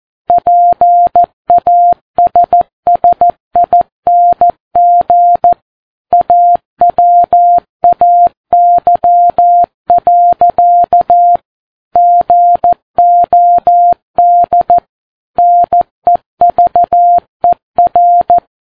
Morse code, heard as a keyed tone, about 14 wpm, 700 Hz. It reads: PASSING AWAY. GOD NEVER
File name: Ham Radio Sat